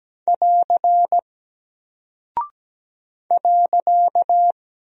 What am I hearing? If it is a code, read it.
Morse code, heard as a keyed tone, about 17 wpm, 700 Hz. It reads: <AR> .
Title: Single Character (Letter:Number:Punctuation:Prosign) Alt 17wpm